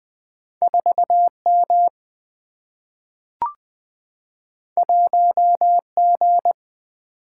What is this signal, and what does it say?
Morse code, heard as a keyed tone, about 20 wpm, 700 Hz. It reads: 4 M 1 G
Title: Single-Letter Single-Number 20wpm